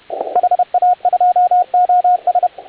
Morse code, heard as a keyed tone, about 31 wpm, 720 Hz, with radio noise. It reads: HA2OS